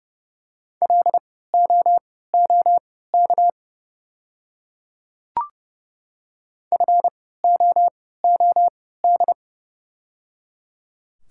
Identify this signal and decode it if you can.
Morse code, heard as a keyed tone, about 29 wpm, spaced as for 17 wpm, 690 Hz. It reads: LOOK FOOD
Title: 3x Character Spacing - Top 300 Words - 30wpm